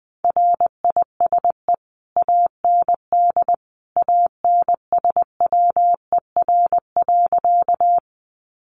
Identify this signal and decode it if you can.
Morse code, heard as a keyed tone, about 19 wpm, 700 Hz. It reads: RISE AND ANSWER.
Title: Sentences from Top 900 Words 20wpm